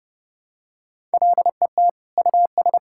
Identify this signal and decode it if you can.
Morse code, heard as a keyed tone, about 32 wpm, 700 Hz. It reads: LET US
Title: Sentences from Top 1600 Words 30wpm